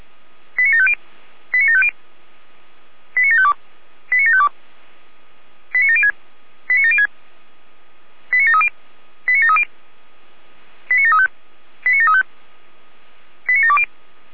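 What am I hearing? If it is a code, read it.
Five-tone selective call (ZVEI), heard as quick runs of five tones, tones of about 80 ms. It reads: 89750 89750 89752 89752 89706 89706 89730 89730 89735 89735 89720
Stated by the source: ZVEI five-tone selective call received by SDR.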